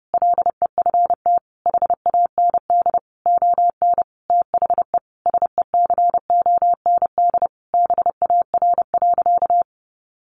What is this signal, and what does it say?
Morse code, heard as a keyed tone, about 30 wpm, 700 Hz. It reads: LEFT HAND ON THE SECOND BAR.
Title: Easy Phrases 30wpm